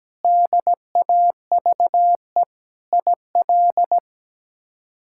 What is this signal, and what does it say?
Morse code, heard as a keyed tone, about 17 wpm, 700 Hz. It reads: DAVE IL
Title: K1USN SST QSO Practice - 17wpm